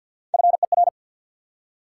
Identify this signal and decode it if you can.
Morse code, heard as a keyed tone, about 48 wpm, 700 Hz. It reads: FER
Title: Top 180 QSO Elements 50wpm